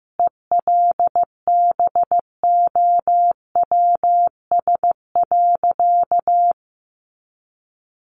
Morse code, heard as a keyed tone, about 15 wpm, 700 Hz. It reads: ELBOWS.
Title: A Sound Of Thunder 15wpm